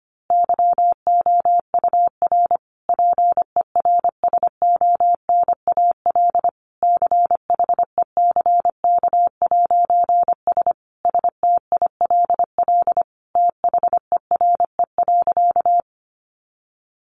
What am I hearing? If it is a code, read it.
Morse code, heard as a keyed tone, about 25 wpm, 700 Hz. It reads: YOUR PERSONAL CHECK'S STILL THERE.
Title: A Sound Of Thunder 25wpm